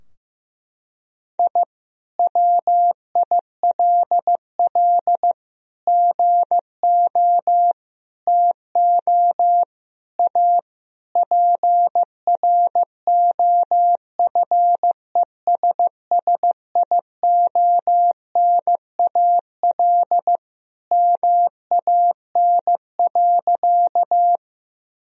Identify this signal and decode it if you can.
Morse code, heard as a keyed tone, about 15 wpm, 700 Hz. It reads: I WILL GO TO A PROFESSIONAL MAN.
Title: Sentences from Top 1000 Words - Review Part 4 - 15wpm